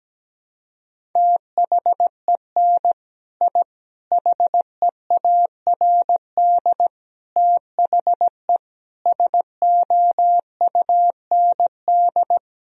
Morse code, heard as a keyed tone, about 17 wpm, 700 Hz. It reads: THEN I HEARD THE SOUND
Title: Sentences from Top 1000 Words - Review Part 1 - 17wpm